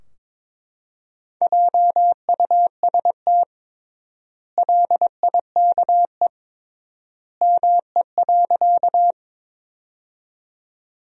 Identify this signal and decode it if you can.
Morse code, heard as a keyed tone, about 22 wpm, 700 Hz. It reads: JUST LIKE ME.
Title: 3x Word-Spacing - Sentences from Top 200 Words - 22wpm